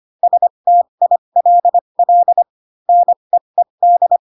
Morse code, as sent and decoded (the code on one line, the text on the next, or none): ... - .. .-.. .-.. / -. . -..
STILL NEED